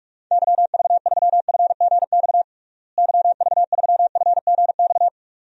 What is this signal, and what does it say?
Morse code, heard as a keyed tone, about 41 wpm, 700 Hz. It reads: YV3FGX YV3FGX
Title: DXCC-100 Call Signs - 6 Characters - Repeated 2x - 45wpm